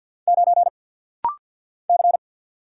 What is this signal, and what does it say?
Morse code, heard as a keyed tone, about 47 wpm, 690 Hz. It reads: 9 C
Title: Single Letter-Number Alt - Rapid-Fire 50wpm